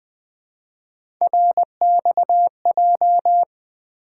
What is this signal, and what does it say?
Morse code, heard as a keyed tone, about 21 wpm, 700 Hz. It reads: RXJ
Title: Three Letter-Number Combinations - 20wpm